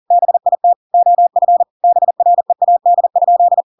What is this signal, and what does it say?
Morse code, heard as a keyed tone, about 40 wpm, 700 Hz. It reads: BIT OF BREAD?
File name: Pinocchio / Pinocchio ch6 40 40 ik1hge